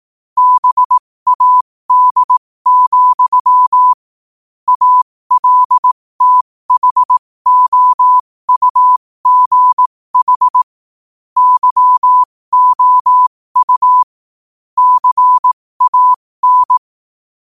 Morse code, heard as a keyed tone, about 18 wpm, spaced as for 15 wpm, 1000 Hz. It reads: BAD, ALTHOUGH YOU CAN
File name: 15wpm